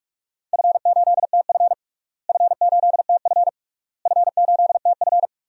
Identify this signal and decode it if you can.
Morse code, heard as a keyed tone, about 48 wpm, 700 Hz. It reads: F8TF F8TF F8TF
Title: DXCC-100 Call Signs - 4 Characters - Repeated 3x - 45wpm